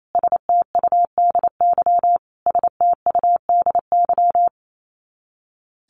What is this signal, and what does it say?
Morse code, heard as a keyed tone, about 28 wpm, 700 Hz. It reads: STUDY STUDY
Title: Top 300 Words Repeated 28wpm